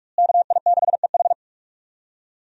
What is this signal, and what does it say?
Morse code, heard as a keyed tone, about 46 wpm, 700 Hz. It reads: KI6EH
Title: USA Call Signs - 5 Characters - Repeated 3x - 45wpm